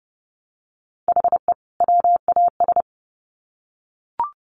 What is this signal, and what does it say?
Morse code, heard as a keyed tone, about 30 wpm, 700 Hz. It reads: HE WAS
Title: Sets of 2 Words 30wpm